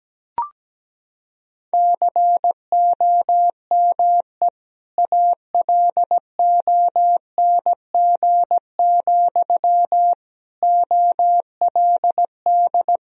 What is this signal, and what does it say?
Morse code, heard as a keyed tone, about 17 wpm, 700 Hz. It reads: COME ALONG, OLD
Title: Sentences from Top 1000 Words - Review Part 2 - 17wpm